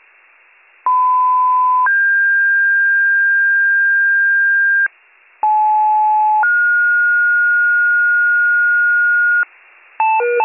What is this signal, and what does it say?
Two-tone page, heard as a one-second tone, then a three-second tone, about 1010 then 1600 Hz, about 860 then 1430 Hz, with noise. It is heard from 0.9 to 9.4 s.